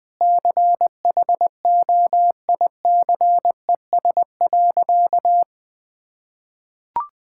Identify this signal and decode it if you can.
Morse code, heard as a keyed tone, about 20 wpm, 700 Hz. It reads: CHOICES.
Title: Sentences with 3-Word Phrases - Spoken Beginning - 20wpm